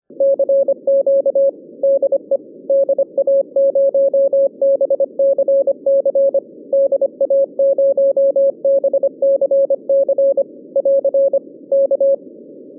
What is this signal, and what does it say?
Morse code, heard as a keyed tone, about 25 wpm, 550 Hz. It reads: CQ DE DA0BCC DA0BCC <AR> K
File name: 2011 bcc-challenge sonntag tape